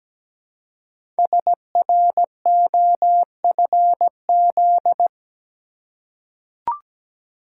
Morse code, heard as a keyed tone, about 17 wpm, 700 Hz. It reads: SROFZ
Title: Code Groups - 5 Letters - 17wpm